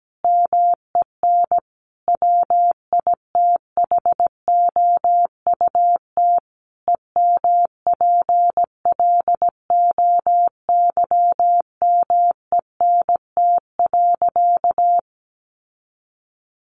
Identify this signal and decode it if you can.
Morse code, heard as a keyed tone, about 17 wpm, 700 Hz. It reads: MEN WITHOUT EMPLOYMENT.
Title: Sentences from Top 1500 Words 17wpm